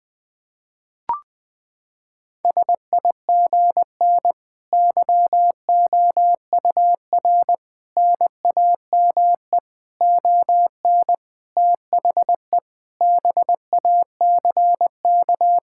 Morse code, heard as a keyed tone, about 21 wpm, 700 Hz. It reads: SIGN YOUR NAME ON THE BACK
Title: Sentences with 5-Word Phrases - 20wpm